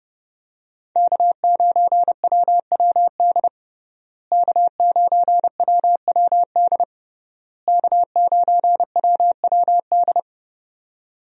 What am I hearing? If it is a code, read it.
Morse code, heard as a keyed tone, about 30 wpm, 700 Hz. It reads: K9WWD K9WWD K9WWD
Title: USA Call Signs - 5 Characters - Repeated 3x - 30wpm